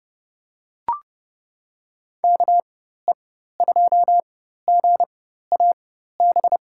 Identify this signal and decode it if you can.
Morse code, heard as a keyed tone, about 32 wpm, spaced as for 14 wpm, 700 Hz. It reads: KE2GAB
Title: USA Call Signs - 6 Characters - 4x Character Spacing - 30wpm